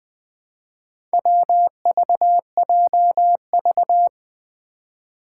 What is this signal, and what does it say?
Morse code, heard as a keyed tone, about 20 wpm, 700 Hz. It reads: WVJV
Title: Four Challenging Letter-Number Combinations 20wpm